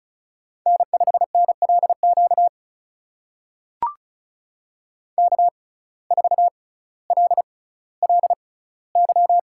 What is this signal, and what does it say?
Morse code, heard as a keyed tone, about 36 wpm, spaced as for 18 wpm, 700 Hz. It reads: N5NLQ K4LLY
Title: USA Call Signs - 5 Characters - 6x Character Spacing - 35wpm